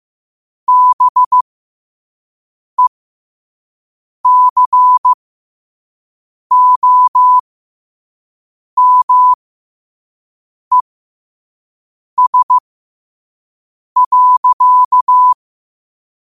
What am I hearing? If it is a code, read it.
Morse code, heard as a keyed tone, about 15 wpm, spaced as for 5 wpm, 1000 Hz. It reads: BECOMES.
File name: qotdfiles / 5wpm